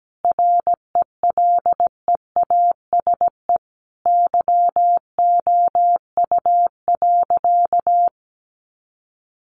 Morse code, heard as a keyed tone, about 17 wpm, 700 Hz. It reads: RELEASE YOU.
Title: Sentences from Top 1100 Words 17wpm